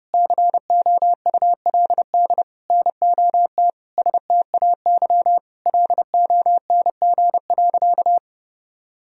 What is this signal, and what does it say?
Morse code, heard as a keyed tone, about 30 wpm, 700 Hz. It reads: COULD NOT STAY LONG.
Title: Sentences from Top 1000 Words - Review Part 3 - 30wpm